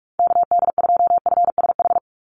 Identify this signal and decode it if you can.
Morse code, heard as a keyed tone, about 44 wpm, 700 Hz. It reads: KD2FSH
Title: Call Signs Repeated 2x - 45wpm